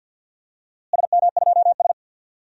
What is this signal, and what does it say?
Morse code, heard as a keyed tone, about 46 wpm, 700 Hz. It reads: SM2S
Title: DXCC-100 Call Signs - 4 Characters - Repeated 3x - 50wpm